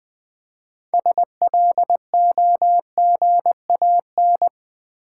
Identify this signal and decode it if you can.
Morse code, heard as a keyed tone, about 20 wpm, 700 Hz. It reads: SLOGAN
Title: 6 Letter Words - First 3 Letters Given - 20wpm